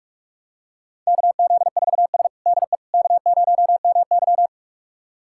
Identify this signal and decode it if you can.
Morse code, heard as a keyed tone, about 45 wpm, 700 Hz. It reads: KZ4S DE K0MY
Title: USA Callsign DE Callsign - 4 Characters - 45wpm